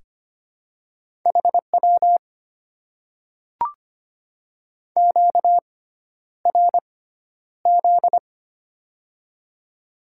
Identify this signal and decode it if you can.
Morse code, heard as a keyed tone, about 25 wpm, spaced as for 11 wpm, 700 Hz. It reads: HW QRZ
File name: Top 90 QSO Elements - 6x Character Spacing - 25wpm